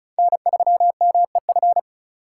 Morse code, heard as a keyed tone, about 34 wpm, 700 Hz. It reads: N3MEF
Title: USA Call Signs - 5 Characters - 4x Character Spacing - 35wpm